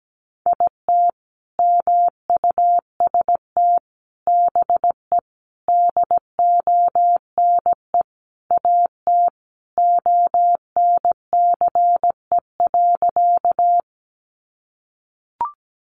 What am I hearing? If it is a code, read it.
Morse code, heard as a keyed tone, about 17 wpm, 700 Hz. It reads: IT MUST BE DONE AT ONCE.
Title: Sentences from Top 500 Words 17wpm